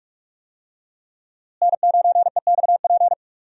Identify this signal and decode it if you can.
Morse code, heard as a keyed tone, about 46 wpm, 700 Hz. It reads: N9EXP
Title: USA Call Signs - 5 Characters - Repeated 3x - 45wpm